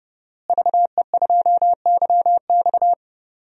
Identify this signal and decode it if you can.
Morse code, heard as a keyed tone, about 30 wpm, 700 Hz. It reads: VE2YX
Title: DXCC-100 Call Signs - 5 Characters - Repeated 2x - 30wpm